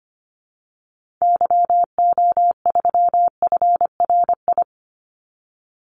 Morse code, heard as a keyed tone, about 25 wpm, 700 Hz. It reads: YO3FRI
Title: Call Signs Repeated 2x - 25wpm